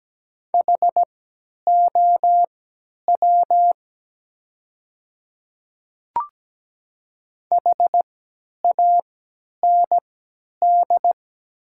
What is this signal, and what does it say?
Morse code, heard as a keyed tone, about 17 wpm, spaced as for 10 wpm, 700 Hz. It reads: HOW HAND
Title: 3x Character Spacing - Top 200 Words - 17wpm